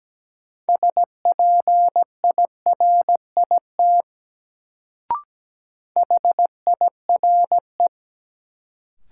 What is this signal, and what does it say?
Morse code, heard as a keyed tone, about 17 wpm, 700 Hz. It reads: SPIRIT HIRE
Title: Sentences from Top 1400 Words - New Word Drill - 17wpm